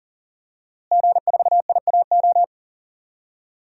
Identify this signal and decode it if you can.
Morse code, heard as a keyed tone, about 37 wpm, 700 Hz. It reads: G4IAO
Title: DXCC-100 Call Signs - 5 Characters - Repeated 2x - 40wpm